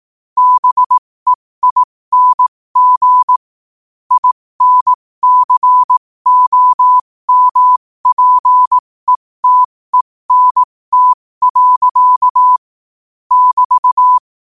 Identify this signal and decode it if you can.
Morse code, heard as a keyed tone, about 18 wpm, spaced as for 15 wpm, 1000 Hz. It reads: BEING INCOMPETENT. <BT>
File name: qotdfiles / 15wpm